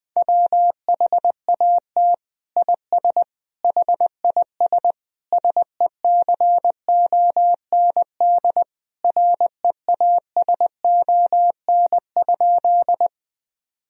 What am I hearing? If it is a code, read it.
Morse code, heard as a keyed tone, about 20 wpm, 700 Hz. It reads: WHAT IS HIS SECOND REASON?
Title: Sentences from Top 1000 Words - Review Part 1 - 20wpm